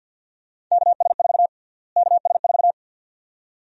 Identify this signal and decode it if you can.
Morse code, heard as a keyed tone, about 51 wpm, 700 Hz. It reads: KS4 KS4
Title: USA Call Sign Prefixes - Repeated 2x - 50wpm